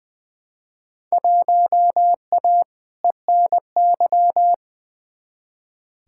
Morse code, heard as a keyed tone, about 20 wpm, 700 Hz. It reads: 1A ENY
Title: Field Day Exchanges Encore - 20wpm